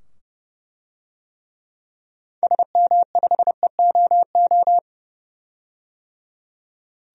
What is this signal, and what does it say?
Morse code, heard as a keyed tone, about 30 wpm, 700 Hz. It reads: SM5EOO
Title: DXCC-100 Call Signs - 6 Characters - 30wpm